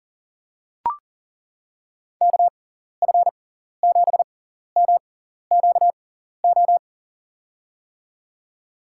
Morse code, heard as a keyed tone, about 40 wpm, spaced as for 14 wpm, 700 Hz. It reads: KF7MQO
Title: USA Call Signs - 6 Characters - 6x Character Spacing - 40wpm